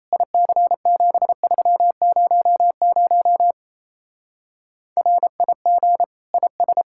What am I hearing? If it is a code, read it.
Morse code, heard as a keyed tone, about 32 wpm, 700 Hz. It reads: IC7300 RIG IS
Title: Speed-Racing Common QSO Phrases - 28wpm